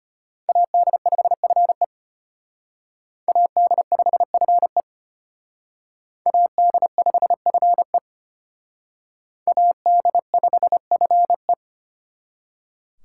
Morse code, heard as a keyed tone, about 29 wpm, 700 Hz. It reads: AD5FE AD5FE AD5FE AD5FE
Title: Speed Racing - USA 2x2 Call Signs 25wpm